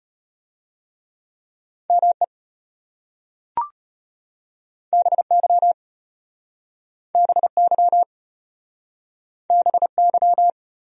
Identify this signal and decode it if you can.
Morse code, heard as a keyed tone, about 34 wpm, 700 Hz. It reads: ME BY BY BY